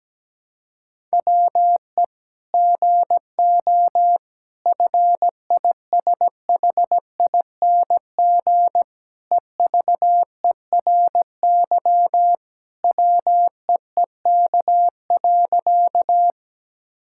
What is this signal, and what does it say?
Morse code, heard as a keyed tone, about 17 wpm, 700 Hz. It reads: WE GO FISHING EVERY WEEK.
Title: Sentences from Top 1000 Words - Review Part 5 - 17wpm